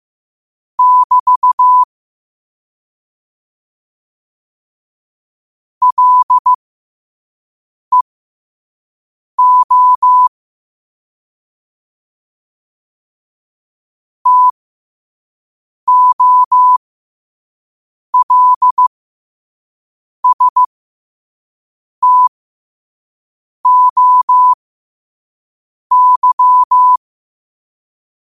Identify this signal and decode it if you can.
Morse code, heard as a keyed tone, about 15 wpm, spaced as for 5 wpm, 1000 Hz. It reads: <BT> LEO TOLSTOY
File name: qotdfiles / 5wpm